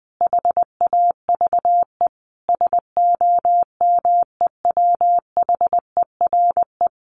Morse code, heard as a keyed tone, about 19 wpm, 700 Hz. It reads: HAVE SOMEWHERE
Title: Sentences from Top 1400 Words 20wpm